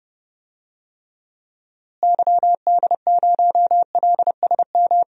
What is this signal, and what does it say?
Morse code, heard as a keyed tone, about 32 wpm, 700 Hz. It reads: YD0LSM
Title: DXCC-100 Call Signs - 6 Characters - Repeated 3x - 30wpm